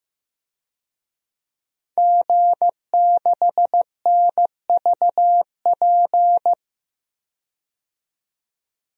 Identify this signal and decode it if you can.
Morse code, heard as a keyed tone, about 16 wpm, 700 Hz. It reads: G6NVP
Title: DXCC-100 Call Signs - 5 Characters - 15wpm